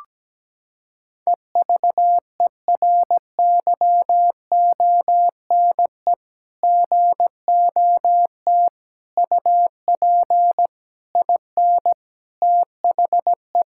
Morse code, heard as a keyed tone, about 17 wpm, 700 Hz. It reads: EVERYONE GOT UP IN THE
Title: Sentences from Top 1000 Words - Review Part 5 - 17wpm